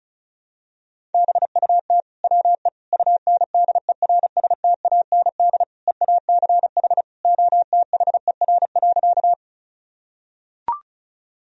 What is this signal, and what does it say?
Morse code, heard as a keyed tone, about 36 wpm, 700 Hz. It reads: BUT WE UNDERSTAND EACH OTHER.
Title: Sentences from Top 1000 Words - Review Part 1 - 35wpm